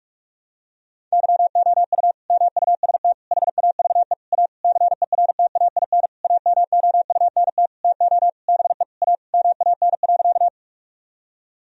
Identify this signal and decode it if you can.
Morse code, heard as a keyed tone, about 42 wpm, 700 Hz. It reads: YOU MUST HAVE A CERTAIN AMOUNT TO BE A MAN.
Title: Sentences from Top 1000 Words - Review Part 4 - 45wpm